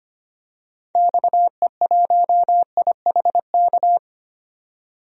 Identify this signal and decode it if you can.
Morse code, heard as a keyed tone, about 25 wpm, 700 Hz. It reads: XE1IHK
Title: DXCC-100 Call Signs - 6 Characters - 25wpm